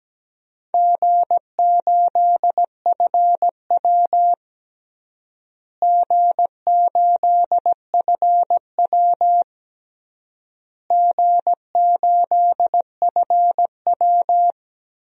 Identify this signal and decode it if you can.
Morse code, heard as a keyed tone, about 17 wpm, 700 Hz. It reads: G8FW G8FW G8FW
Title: DXCC-100 Call Signs - 4 Characters - Repeated 3x - 17wpm